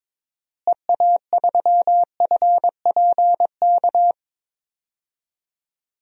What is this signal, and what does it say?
Morse code, heard as a keyed tone, about 21 wpm, 700 Hz. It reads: EA3FPK
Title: DXCC-100 Call Signs - 6 Characters - Repeated 2x - 22wpm